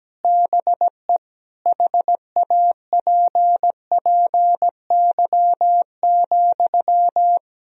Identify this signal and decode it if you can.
Morse code, heard as a keyed tone, about 17 wpm, 700 Hz. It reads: BE HAPPY,
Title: Sentences from Top 1000 Words - Review Part 2 - 17wpm